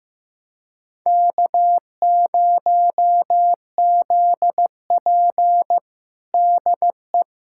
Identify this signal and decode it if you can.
Morse code, heard as a keyed tone, about 15 wpm, 700 Hz. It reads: K0ZP DE
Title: USA Callsign DE Callsign - 4 Characters - 15wpm